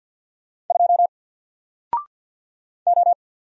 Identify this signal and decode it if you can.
Morse code, heard as a keyed tone, about 54 wpm, 700 Hz. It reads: 2 O
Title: Single Character (Letter:Number:Punctuation:Prosign) 50wpm